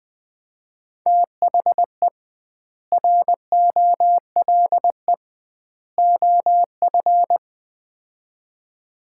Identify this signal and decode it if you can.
Morse code, heard as a keyed tone, about 20 wpm, 700 Hz. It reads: THE ROLE OF
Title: 2x Word-Spacing - Sets of 3 Words - 20wpm